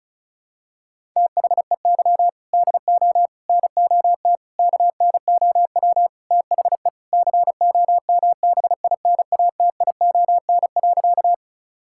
Morse code, heard as a keyed tone, about 34 wpm, 700 Hz. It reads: THEY DO NOT KNOW THE COMBINATION.
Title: Sentences from Top 1000 Words - Review Part 6 - 35wpm